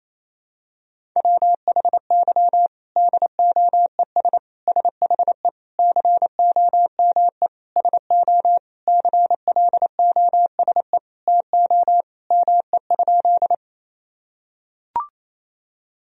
Morse code, heard as a keyed tone, about 28 wpm, 700 Hz. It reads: WHY DOES SHE COME SO CLOSE TO ME?
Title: Sentences from Top 1000 Words - Review Part 1 - 28wpm